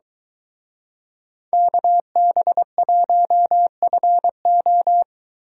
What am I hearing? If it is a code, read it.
Morse code, heard as a keyed tone, about 23 wpm, 700 Hz. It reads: KB1FO